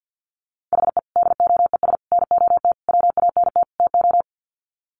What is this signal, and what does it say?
Morse code, heard as a keyed tone, about 50 wpm, 700 Hz. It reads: HE DOES NOT WANT TO
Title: Sets of 5 Words 50wpm